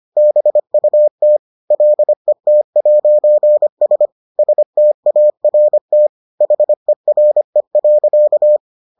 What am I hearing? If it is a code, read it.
Morse code, heard as a keyed tone, about 25 wpm, 600 Hz. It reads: BUT LET'S START HERE.